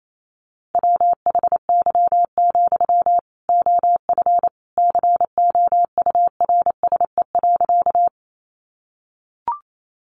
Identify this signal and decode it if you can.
Morse code, heard as a keyed tone, about 28 wpm, 700 Hz. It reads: WHY, OF COURSE.
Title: Sentences from Top 500 Words 28wpm